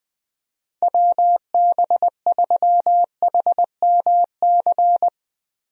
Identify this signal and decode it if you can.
Morse code, heard as a keyed tone, about 20 wpm, 700 Hz. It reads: WB3HMC
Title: USA Call Signs - 6 Characters - Repeated 2x - 20wpm